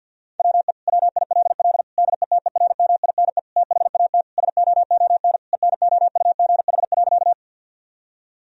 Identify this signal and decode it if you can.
Morse code, heard as a keyed tone, about 50 wpm, 700 Hz. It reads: WE WILL DETERMINE THAT SOON ENOUGH.
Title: Sentences from Top 1000 Words - Review Part 4 - 50wpm